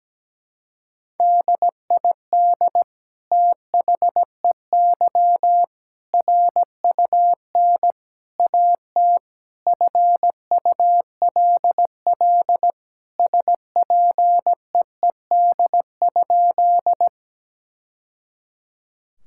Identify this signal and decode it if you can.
Morse code, heard as a keyed tone, about 17 wpm, 700 Hz. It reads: DID THEY RUN AT FULL SPEED?
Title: Sentences from Top 1000 Words - Review Part 5 - 17wpm